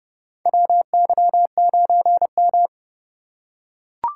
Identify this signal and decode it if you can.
Morse code, heard as a keyed tone, about 30 wpm, 700 Hz. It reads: WY9M
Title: USA Call Signs - 3x Character Spacing - 30wpm